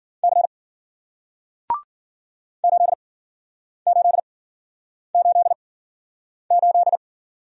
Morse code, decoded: C 8888

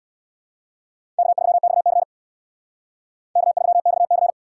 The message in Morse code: K3XC K3XC

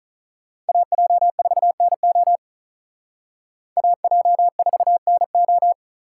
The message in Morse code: AJ4NO AJ4NO

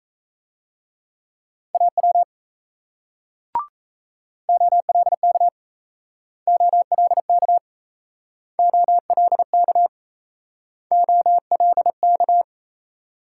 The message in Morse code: AW OLK OLK OLK OLK

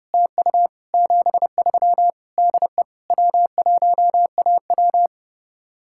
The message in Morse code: TU 73 DE W1AW

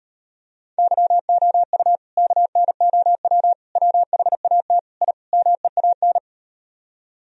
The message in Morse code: YOU KNOW WHAT I MEAN

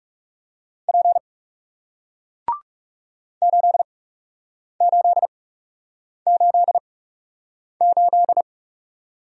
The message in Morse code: P 8888

